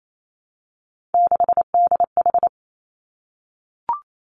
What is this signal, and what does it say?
Morse code, heard as a keyed tone, about 28 wpm, 700 Hz. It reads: 6DH
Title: Three Challenging Letter-Number Combinations 28wpm